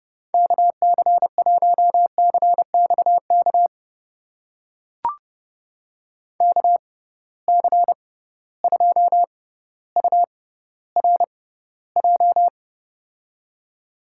Morse code, heard as a keyed tone, about 31 wpm, spaced as for 15 wpm, 700 Hz. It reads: KC1CXK KC2URJ
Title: USA Call Signs - 6 Characters - 6x Character Spacing - 30wpm